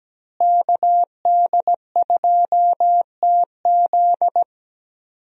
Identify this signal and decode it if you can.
Morse code, heard as a keyed tone, about 17 wpm, 700 Hz. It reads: KD2TZ
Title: USA Call Signs - 5 Characters - Repeated 3x - 17wpm